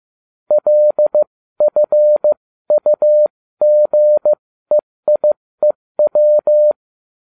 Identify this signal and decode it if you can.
Morse code, heard as a keyed tone, about 15 wpm, spaced as for 13 wpm, 600 Hz. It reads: LFUGEIEW